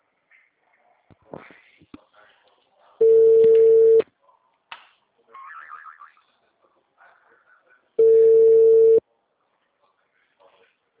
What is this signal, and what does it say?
Ringback tone (China, 450 Hz).
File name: out- -2003- - - .95515